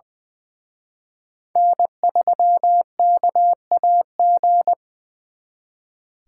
Morse code, heard as a keyed tone, about 23 wpm, 700 Hz. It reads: N3KAG
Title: Speed Racing - USA 1x3 Call Signs 20wpm